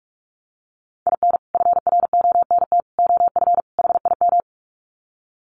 Morse code, heard as a keyed tone, about 44 wpm, 700 Hz. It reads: IN FRONT OF HIM
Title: Sets of 4 Words 45wpm